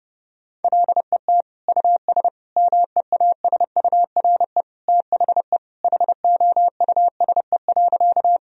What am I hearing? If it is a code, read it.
Morse code, heard as a keyed tone, about 32 wpm, 700 Hz. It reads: LET US MEASURE THE HOUSE.
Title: Sentences from Top 1000 Words - Review Part 2 - 30wpm